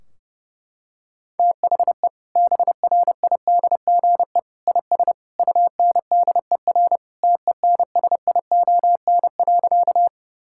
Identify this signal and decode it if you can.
Morse code, heard as a keyed tone, about 32 wpm, 700 Hz. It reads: THE BRIDGE IS UNDER TENSION.
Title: Sentences from Top 1600 Words 30wpm